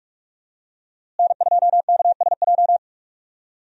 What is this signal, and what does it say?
Morse code, heard as a keyed tone, about 46 wpm, 700 Hz. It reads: N2KSJ
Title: USA Call Signs - 5 Characters - Repeated 3x - 45wpm